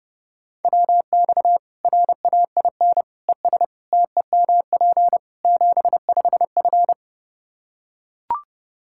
Morse code, heard as a keyed tone, about 30 wpm, 700 Hz. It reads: WX RAIN ES TEMP 75F
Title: Practice QSO - Exchange 2 - Info - 30wpm